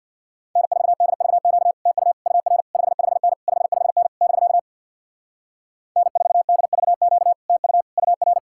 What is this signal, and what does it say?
Morse code, heard as a keyed tone, about 62 wpm, 700 Hz. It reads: N4DVQ TU UR 55N 55N <BK> N4DVQ TU UR
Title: Speed-Racing Common QSO Phrases - 50wpm